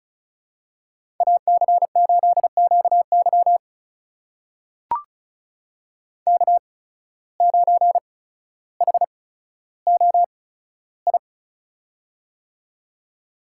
Morse code, heard as a keyed tone, about 36 wpm, spaced as for 15 wpm, 700 Hz. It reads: AC8QY K9HOI
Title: USA Call Signs - 8x Character Spacing - 35wpm